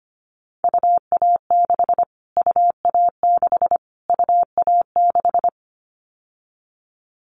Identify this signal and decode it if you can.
Morse code, heard as a keyed tone, about 25 wpm, 700 Hz. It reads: UA6 UA6 UA6
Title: Callsign Prefixes for Top 25 DXCC - Repeated 3x - 25wpm